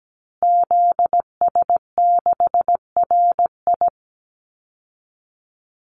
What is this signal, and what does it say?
Morse code, heard as a keyed tone, about 17 wpm, 700 Hz. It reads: ZS6RI
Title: Call Signs Repeated 3x - 17wpm